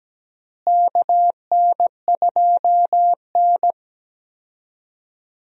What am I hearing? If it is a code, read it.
Morse code, heard as a keyed tone, about 17 wpm, 700 Hz. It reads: KN2N
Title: USA Call Signs - 4 Characters - Repeated 3x - 17wpm